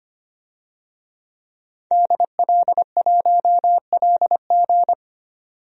Morse code, heard as a keyed tone, about 25 wpm, 700 Hz. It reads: DL1LG
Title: DXCC-100 Call Signs - 5 Characters - 25wpm